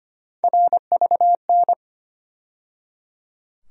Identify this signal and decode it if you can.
Morse code, heard as a keyed tone, about 25 wpm, 700 Hz. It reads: RVN
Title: USA Call Sign Postfixes - 3 Letters 25wpm